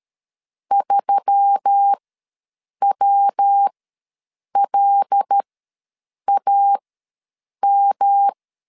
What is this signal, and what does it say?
Morse code, heard as a keyed tone, about 13 wpm, spaced as for 7 wpm, 800 Hz. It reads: 3WLAM
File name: LEZIONI / 5 LBVC / 5 L ws 12wpm 2958